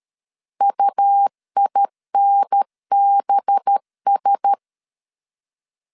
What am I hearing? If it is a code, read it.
Morse code, heard as a keyed tone, about 13 wpm, 800 Hz. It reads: UINBS
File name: CW / DET / 1 6 1500C S A 12wpm 3440